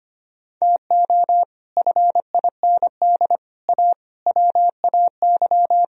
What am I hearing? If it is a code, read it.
Morse code, heard as a keyed tone, about 32 wpm, 700 Hz. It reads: TO FIND A WAY